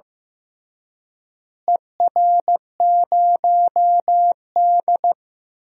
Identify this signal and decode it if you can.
Morse code, heard as a keyed tone, about 15 wpm, 700 Hz. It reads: ER0D
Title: DXCC-100 Call Signs - 4 Characters - Repeated 3x - 15wpm